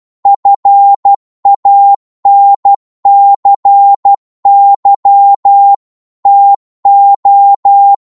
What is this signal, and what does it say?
Morse code, heard as a keyed tone, about 12 wpm, 800 Hz. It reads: FANCY TO
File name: Morse / psg-12wpm